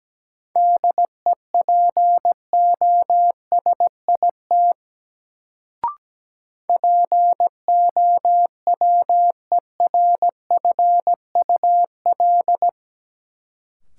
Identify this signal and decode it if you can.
Morse code, heard as a keyed tone, about 17 wpm, 700 Hz. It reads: DEPOSIT POWERFUL
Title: Sentences from Top 1300 Words - New Word Drill Alt - 17wpm